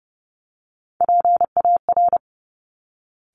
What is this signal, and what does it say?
Morse code, heard as a keyed tone, about 30 wpm, 700 Hz. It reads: PAR
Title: Top Three Letter Combinations in Words 30wpm